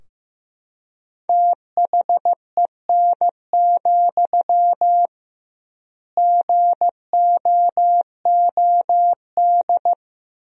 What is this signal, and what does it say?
Morse code, heard as a keyed tone, about 15 wpm, 700 Hz. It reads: THEN, GOOD
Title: 2x Word-Spacing - Sentences from Top 300 Words - 15wpm